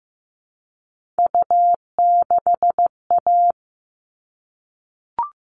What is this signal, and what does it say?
Morse code, heard as a keyed tone, about 15 wpm, 700 Hz. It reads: U6A
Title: All Cut Numbers - 3-Digit 15wpm